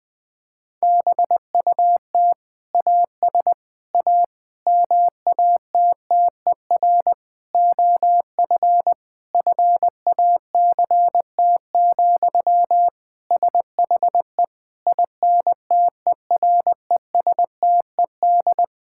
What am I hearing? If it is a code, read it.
Morse code, heard as a keyed tone, about 21 wpm, 700 Hz. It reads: BUT AS A MATTER OF FACT, SHE INTERESTED
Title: Sentences from Top 1000 Words - Review Part 6 - 20wpm